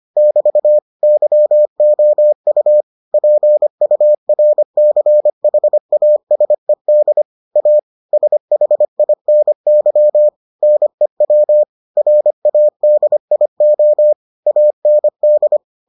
Morse code, heard as a keyed tone, about 25 wpm, 600 Hz. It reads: <BT> YOU PURCHASED A SHINY NEW RADIO AND